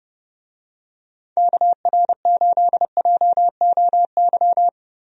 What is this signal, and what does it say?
Morse code, heard as a keyed tone, about 30 wpm, 700 Hz. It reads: KR8JOY
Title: USA Call Signs - 6 Characters - Repeated 3x - 30wpm